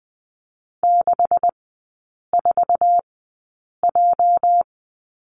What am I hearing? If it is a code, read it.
Morse code, heard as a keyed tone, about 20 wpm, spaced as for 8 wpm, 700 Hz. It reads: 64J
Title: 205 Lesson 33e - Letter j - Three Characters 20wpm